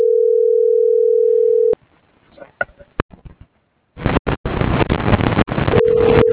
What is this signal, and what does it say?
Ringback tone (North America), 440 + 480 Hz.